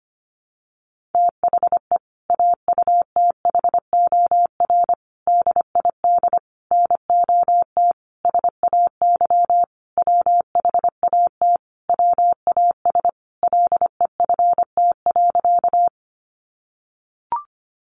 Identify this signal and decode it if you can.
Morse code, heard as a keyed tone, about 25 wpm, 700 Hz. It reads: THE AUTHOR DID NOT SAY WHAT WAS LEFT.
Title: Sentences from Top 1200 Words 25wpm